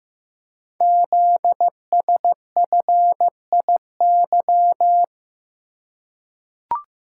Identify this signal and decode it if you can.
Morse code, heard as a keyed tone, about 15 wpm, 700 Hz. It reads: ZSFIY
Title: Code Groups - 5 Letters - 15wpm